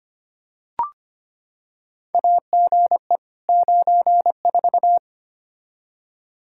Morse code, heard as a keyed tone, about 25 wpm, 700 Hz. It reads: AGE 94
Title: K1USN SST QSO Practice 25wpm